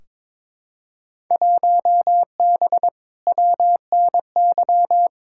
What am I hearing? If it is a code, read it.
Morse code, heard as a keyed tone, about 23 wpm, 700 Hz. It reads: 1B WNY
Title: Field Day Exchanges - 22wpm